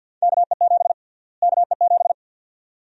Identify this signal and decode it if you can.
Morse code, heard as a keyed tone, about 52 wpm, 700 Hz. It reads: KE7 KE7
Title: USA Call Sign Prefixes - Repeated 2x - 50wpm